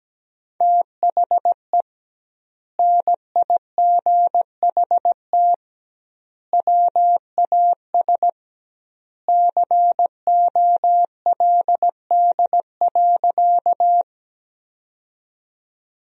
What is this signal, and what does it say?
Morse code, heard as a keyed tone, about 17 wpm, 700 Hz. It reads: THE NIGHT WAS COLD.
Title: 2x Word-Spacing - Sentences from Top 400 Words - 17wpm